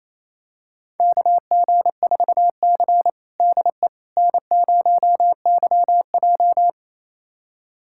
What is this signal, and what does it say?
Morse code, heard as a keyed tone, about 28 wpm, 700 Hz. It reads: KG4C DE N0YJ
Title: USA Callsign DE Callsign - 4 Characters - 28wpm